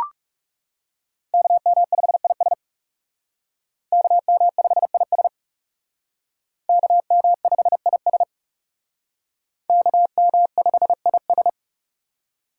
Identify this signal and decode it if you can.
Morse code, heard as a keyed tone, about 35 wpm, 700 Hz. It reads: KM5IS KM5IS KM5IS KM5IS
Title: Speed Racing - USA 2x2 Call Signs 30wpm